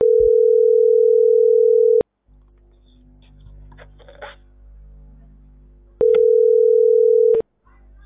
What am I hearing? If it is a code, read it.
Ringback tone (North America), 440 + 480 Hz.